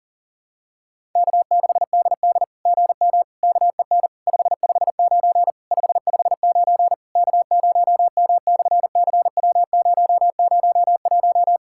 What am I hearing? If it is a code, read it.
Morse code, heard as a keyed tone, about 38 wpm, 700 Hz. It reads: K6DD GM KEN 559 559 K0M/CW001
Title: SOTA QSO Practice 40wpm